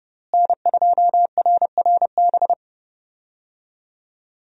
Morse code, heard as a keyed tone, about 28 wpm, 700 Hz. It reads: N2RRB
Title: USA Call Signs - 5 Characters - Repeated 3x - 30wpm